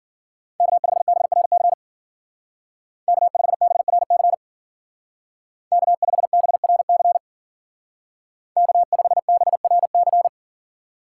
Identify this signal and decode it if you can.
Morse code, heard as a keyed tone, about 49 wpm, 700 Hz. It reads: K5BRC K5BRC K5BRC K5BRC
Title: Speed Racing - USA 1x3 Call Signs 40wpm